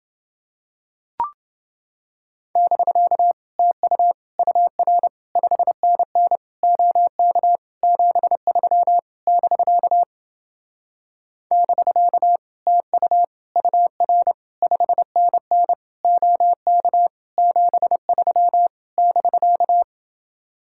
Morse code, heard as a keyed tone, about 31 wpm, 700 Hz. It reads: <BK> TU UR 5NN OK 73 <BK> <BK> TU UR 5NN OK 73 <BK>